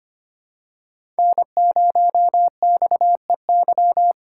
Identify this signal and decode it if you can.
Morse code, heard as a keyed tone, about 25 wpm, 700 Hz. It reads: N0XEY
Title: USA Call Signs - 5 Characters - 3x Character Spacing - 25wpm